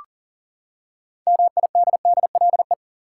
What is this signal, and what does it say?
Morse code, heard as a keyed tone, about 40 wpm, 700 Hz. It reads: MIDDLE
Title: Six Letter Words - 40wpm